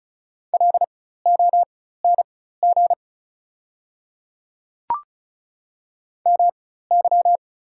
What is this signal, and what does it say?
Morse code, heard as a keyed tone, about 36 wpm, spaced as for 16 wpm, 690 Hz. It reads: LONG MY
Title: 4x Character Spacing - Top 200 Words - 35wpm